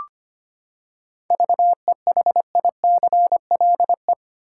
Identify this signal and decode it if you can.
Morse code, heard as a keyed tone, about 25 wpm, 700 Hz. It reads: VEHICLE
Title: Seven Letter Words - 25wpm